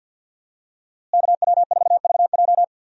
Morse code, heard as a keyed tone, about 52 wpm, 700 Hz. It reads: KW4VJ
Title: USA Call Signs - 5 Characters - Repeated 3x - 50wpm